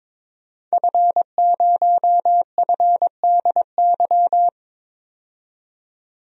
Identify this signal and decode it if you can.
Morse code, heard as a keyed tone, about 22 wpm, 700 Hz. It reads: F0FDY
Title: DXCC-100 Call Signs - 5 Characters - 22wpm